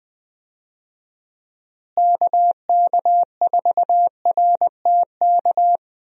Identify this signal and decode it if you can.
Morse code, heard as a keyed tone, about 21 wpm, 700 Hz. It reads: KK4RTK
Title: USA Call Signs - 6 Characters - Repeated 2x - 20wpm